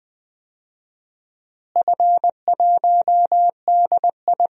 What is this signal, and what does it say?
Morse code, heard as a keyed tone, about 20 wpm, 700 Hz. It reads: F1DI
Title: DXCC-100 Call Signs - 4 Characters - Repeated 3x - 20wpm